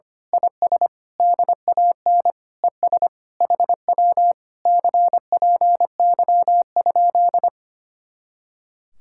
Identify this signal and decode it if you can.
Morse code, heard as a keyed tone, about 24 wpm, 700 Hz. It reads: IS DAN ES HW CPY?
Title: Speed-Racing Common QSO Phrases Encore - 25wpm